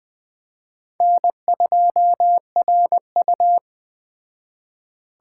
Morse code, heard as a keyed tone, about 21 wpm, 700 Hz. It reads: N2RU
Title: Speed Racing - USA 1x2 Call Signs 17wpm